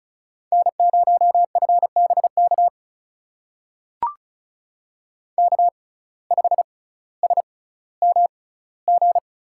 Morse code, heard as a keyed tone, about 36 wpm, spaced as for 18 wpm, 700 Hz. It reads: N0FBK K5SMG
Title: USA Call Signs - 5 Characters - 6x Character Spacing - 35wpm